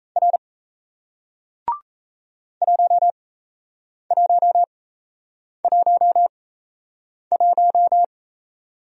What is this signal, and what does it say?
Morse code, heard as a keyed tone, about 37 wpm, spaced as for 8 wpm, 700 Hz. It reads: R 1111